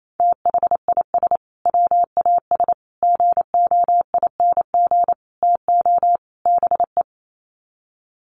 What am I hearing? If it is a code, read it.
Morse code, heard as a keyed tone, about 28 wpm, 700 Hz. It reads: THIS WAS GOING TO BE
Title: Sets of 5 Words Encore2 28wpm